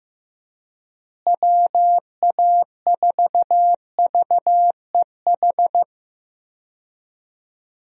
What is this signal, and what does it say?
Morse code, heard as a keyed tone, about 15 wpm, 700 Hz. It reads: WA4VEH
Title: USA Call Signs - 6 Characters - Repeated 2x - 15wpm